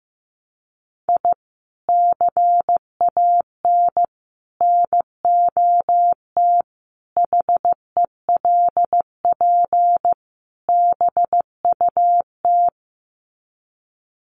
Morse code, heard as a keyed tone, about 15 wpm, 700 Hz. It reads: I CAN NOT HELP BUT
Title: Sets of 5 Words Encore2 15wpm